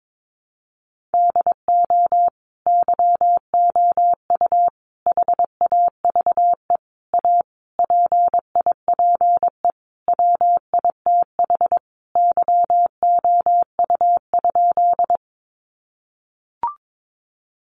Morse code, heard as a keyed tone, about 22 wpm, 700 Hz. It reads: DO YOU HAVE A PIPE WITH YOU?
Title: Sentences from Top 1500 Words 22wpm